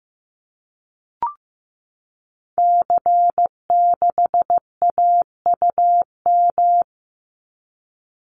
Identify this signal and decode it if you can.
Morse code, heard as a keyed tone, about 15 wpm, 700 Hz. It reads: C6AUM
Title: Call Signs Repeated 3x - 15wpm